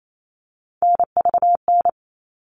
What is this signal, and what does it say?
Morse code, heard as a keyed tone, about 28 wpm, 700 Hz. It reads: NVN
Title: All Cut Numbers - 3-Digit 28wpm